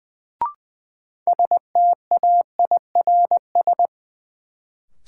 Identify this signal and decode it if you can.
Morse code, heard as a keyed tone, about 20 wpm, 700 Hz. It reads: STAIRS
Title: Sets of 5 Words Encore - New Word Drill - 20wpm